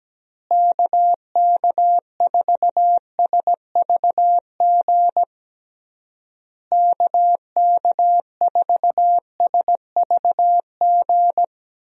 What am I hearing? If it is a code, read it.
Morse code, heard as a keyed tone, about 17 wpm, 700 Hz. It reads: KK4SVG KK4SVG
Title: USA Call Signs - 6 Characters - Repeated 3x - 17wpm